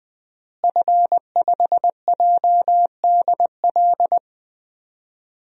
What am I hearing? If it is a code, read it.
Morse code, heard as a keyed tone, about 20 wpm, 700 Hz. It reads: F5JDL
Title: DXCC-100 Call Signs - 5 Characters - Repeated 2x - 20wpm